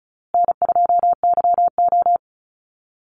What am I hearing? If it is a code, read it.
Morse code, heard as a keyed tone, about 34 wpm, 700 Hz. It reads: N2YO
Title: Easy Call Signs Repeated 2x - 35wpm